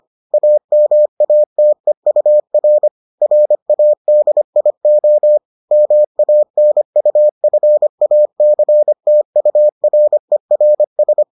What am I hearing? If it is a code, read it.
Morse code, heard as a keyed tone, about 25 wpm, 600 Hz. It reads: AMATEUR RADIO MANUFACTURERS